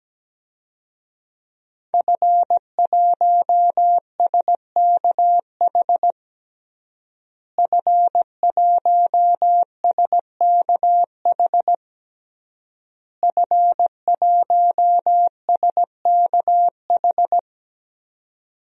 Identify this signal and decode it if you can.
Morse code, heard as a keyed tone, about 17 wpm, 700 Hz. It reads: F1SKH F1SKH F1SKH
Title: DXCC-100 Call Signs - 5 Characters - Repeated 3x - 17wpm